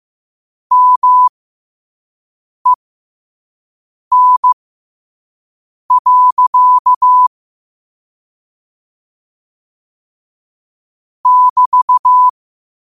Morse code, heard as a keyed tone, about 15 wpm, spaced as for 5 wpm, 1000 Hz. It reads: MEN. <BT>
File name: qotdfiles / 5wpm